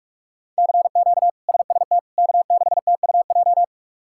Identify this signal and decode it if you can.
Morse code, heard as a keyed tone, about 43 wpm, 700 Hz. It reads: CQ SST K6TUJ
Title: K1USN SST QSO Practice - 45wpm